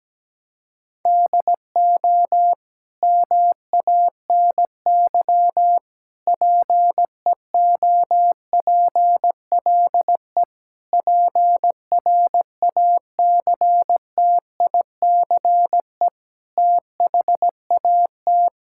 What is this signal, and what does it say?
Morse code, heard as a keyed tone, about 17 wpm, 700 Hz. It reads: DO MANY PEOPLE PRACTICE THAT
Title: Sentences from Top 1000 Words - Review Part 6 - 17wpm